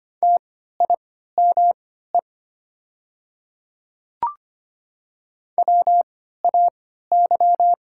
Morse code, heard as a keyed tone, about 25 wpm, spaced as for 14 wpm, 700 Hz. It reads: TIME WAY
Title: 3x Character Spacing - Top 100 Words - 25wpm